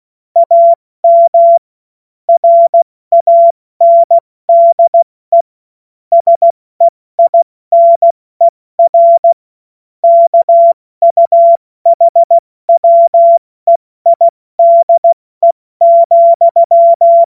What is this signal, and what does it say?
Morse code, heard as a keyed tone, about 16 wpm, 680 Hz. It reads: AM RANDE SEINER KUHWEIDE,
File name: Morsekurs / Witze / Witz 17 Juni Mai 2025 70 F80